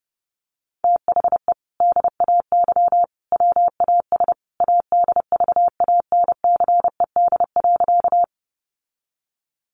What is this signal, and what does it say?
Morse code, heard as a keyed tone, about 30 wpm, 700 Hz. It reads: THE DAY WAS ADVANCED.
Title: Sentences from Top 1200 Words 30wpm